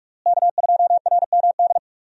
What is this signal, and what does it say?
Morse code, heard as a keyed tone, about 46 wpm, 700 Hz. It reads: K2RMD
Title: USA Call Signs - 5 Characters - Repeated 2x - 45wpm